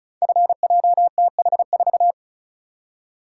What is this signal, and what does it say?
Morse code, heard as a keyed tone, about 36 wpm, 690 Hz. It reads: FJTH4
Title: Code Groups - 5 Letter:Numbers - 35wpm